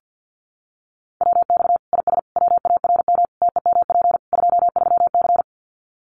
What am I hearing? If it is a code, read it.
Morse code, heard as a keyed tone, about 50 wpm, 700 Hz. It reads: WX IS WARM TEMP 23C
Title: Common QSO Phrases 50wpm